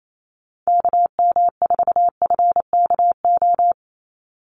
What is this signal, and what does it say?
Morse code, heard as a keyed tone, about 28 wpm, 700 Hz. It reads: KM4FKO
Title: USA Call Signs 28wpm